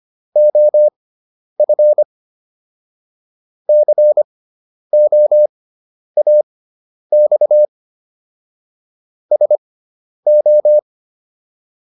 Morse code, heard as a keyed tone, about 25 wpm, spaced as for 10 wpm, 600 Hz. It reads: OF COAX SO